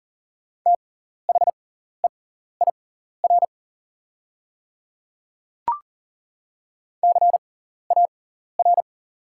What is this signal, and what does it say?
Morse code, heard as a keyed tone, about 40 wpm, spaced as for 14 wpm, 690 Hz. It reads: THEIR CAR
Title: 6x Character Spacing - Top 500 Words - 40wpm